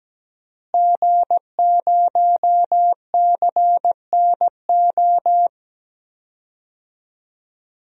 Morse code, heard as a keyed tone, about 17 wpm, 700 Hz. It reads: G0CNO
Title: DXCC-100 Call Signs - 5 Characters - Repeated 3x - 17wpm